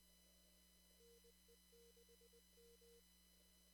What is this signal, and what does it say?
Morse code, heard as a keyed tone, about 20 wpm, 470 Hz. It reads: NE6M
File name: CW / kt5j wpx cw-op-k5pi